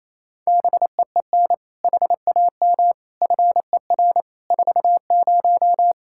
Morse code, heard as a keyed tone, about 28 wpm, 700 Hz. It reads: BEEN HAM FER 40
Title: Practice QSO - Exchange 3 - More Info - 28wpm